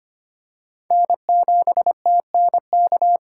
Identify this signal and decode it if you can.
Morse code, heard as a keyed tone, about 25 wpm, 700 Hz. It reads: N7TNK
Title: USA Call Signs - 5 Characters - 3x Character Spacing - 25wpm